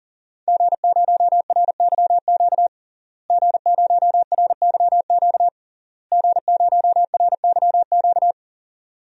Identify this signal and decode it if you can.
Morse code, heard as a keyed tone, about 40 wpm, 700 Hz. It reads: G0RYQ G0RYQ G0RYQ
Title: DXCC-100 Call Signs - 5 Characters - Repeated 3x - 40wpm